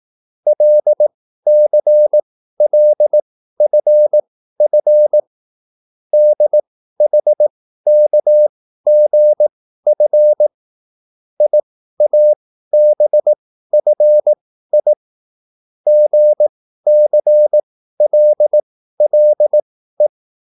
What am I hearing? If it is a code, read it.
Morse code, heard as a keyed tone, about 18 wpm, spaced as for 13 wpm, 600 Hz. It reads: LCLFF DHKGF IABFI GCLLE